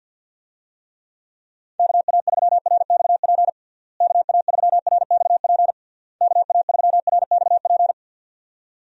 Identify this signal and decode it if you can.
Morse code, heard as a keyed tone, about 46 wpm, 700 Hz. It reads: KA3RXP KA3RXP KA3RXP
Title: USA Call Signs - 6 Characters - Repeated 3x - 50wpm